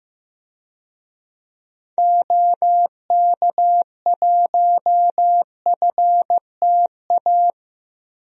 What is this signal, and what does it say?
Morse code, heard as a keyed tone, about 16 wpm, 700 Hz. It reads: OK1FTA
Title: DXCC-100 Call Signs - 6 Characters - 15wpm